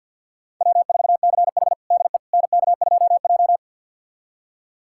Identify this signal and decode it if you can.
Morse code, heard as a keyed tone, about 52 wpm, 700 Hz. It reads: W4CH DE NK2J
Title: USA Callsign DE Callsign - 4 Characters - 50wpm